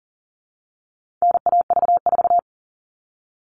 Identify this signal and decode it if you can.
Morse code, heard as a keyed tone, about 37 wpm, 700 Hz. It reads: NAV4
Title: All Cut Numbers - 4-Digit 40wpm